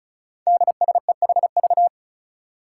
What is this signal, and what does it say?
Morse code, heard as a keyed tone, about 34 wpm, 700 Hz. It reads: DSEHV
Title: Five Challenging Letter-Number Combinations 35wpm